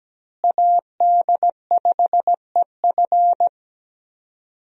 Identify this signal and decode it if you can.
Morse code, heard as a keyed tone, about 17 wpm, 700 Hz. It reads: AD5EF
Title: USA Call Signs - 5 Characters - Repeated 2x - 17wpm